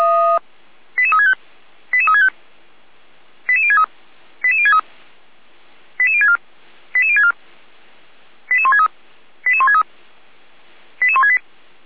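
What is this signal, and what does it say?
Five-tone selective call (ZVEI), heard as quick runs of five tones, tones of about 70 ms. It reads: 88365 88365 88063 88063 88064 88064 88163 88163 88168
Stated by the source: ZVEI five-tone selective call on 455, received by SDR.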